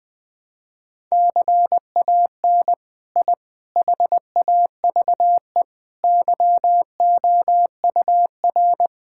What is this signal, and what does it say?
Morse code, heard as a keyed tone, about 20 wpm, 700 Hz. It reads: CAN I HAVE YOUR
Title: Sentences from Top 1000 Words - Review Part 5 - 20wpm